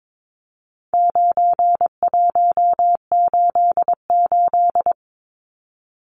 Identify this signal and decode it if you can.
Morse code, heard as a keyed tone, about 22 wpm, 700 Hz. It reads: 9188
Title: Numbers - 4-Digit 22wpm